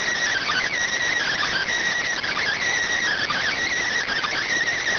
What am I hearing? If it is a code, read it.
Emergency Alert System (SAME) tones, heard as harsh buzzing bursts. It is heard from 0.3 to 4.4 s.